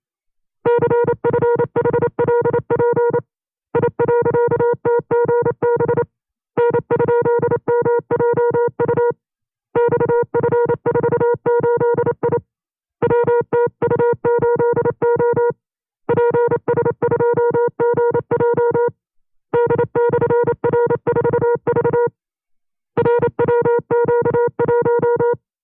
Morse code, heard as a keyed tone, about 28 wpm, 480 Hz. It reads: CFHLP I.TGB N?MJU XF48I WTU8O PS2GJ D/R4V RWQ1
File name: FreeFiles / CW-practice / 25WPM-side1